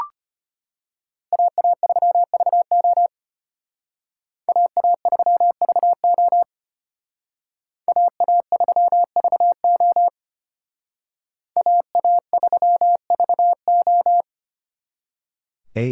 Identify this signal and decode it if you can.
Morse code, heard as a keyed tone, about 32 wpm, 700 Hz. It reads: AA3VO AA3VO AA3VO AA3VO